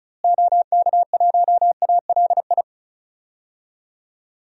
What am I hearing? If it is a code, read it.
Morse code, heard as a keyed tone, about 36 wpm, 700 Hz. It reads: OK1ALI
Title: DXCC-100 Call Signs - 6 Characters - Repeated 3x - 35wpm